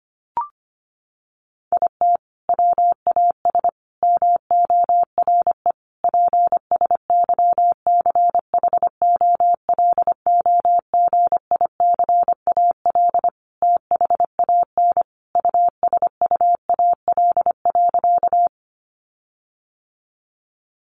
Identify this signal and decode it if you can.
Morse code, heard as a keyed tone, about 25 wpm, 700 Hz. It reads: IT WAS MORE PSYCHOLOGICAL THAN USUAL.
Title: Sentences from Top 1400 Words 25wpm